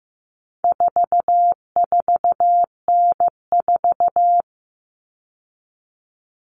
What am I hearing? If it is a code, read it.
Morse code, heard as a keyed tone, about 15 wpm, 700 Hz. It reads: 44N4
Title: All Cut Numbers - 4-Digit 15wpm